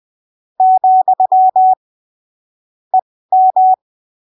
Morse code, heard as a keyed tone, about 20 wpm, spaced as for 15 wpm, 750 Hz. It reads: , EM
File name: arquivos / texto / texto 22 20ppm